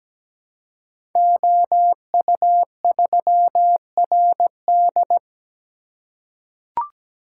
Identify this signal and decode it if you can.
Morse code, heard as a keyed tone, about 17 wpm, 700 Hz. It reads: OU3RD
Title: Code Groups - 5 Letter:Numbers - 17wpm